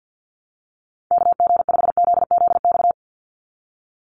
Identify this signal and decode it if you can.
Morse code, heard as a keyed tone, about 48 wpm, 700 Hz. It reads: KG5ZZX
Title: USA Call Signs 50wpm